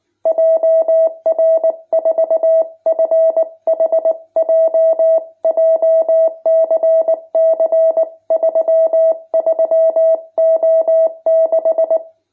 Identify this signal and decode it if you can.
Morse code, heard as a keyed tone, about 19 wpm, 650 Hz. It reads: JR4FHJ JCC 33O6
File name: jh8cbh / 2021cw